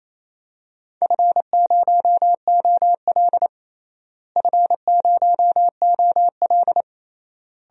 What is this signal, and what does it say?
Morse code, heard as a keyed tone, about 28 wpm, 700 Hz. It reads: F0OL F0OL
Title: DXCC-100 Call Signs - 4 Characters - Repeated 3x - 28wpm